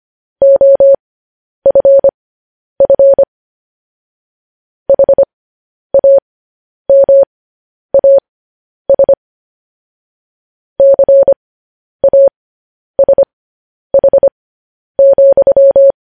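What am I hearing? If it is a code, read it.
Morse code, heard as a keyed tone, about 25 wpm, spaced as for 10 wpm, 550 Hz. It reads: OFF HAMAS CASH,
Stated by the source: Morse code news headlines practice transmission.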